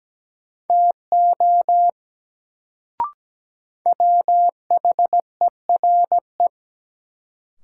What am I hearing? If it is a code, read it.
Morse code, heard as a keyed tone, about 17 wpm, 700 Hz. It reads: TO WHERE
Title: Sets of 2 Words - New Word Drill - 17wpm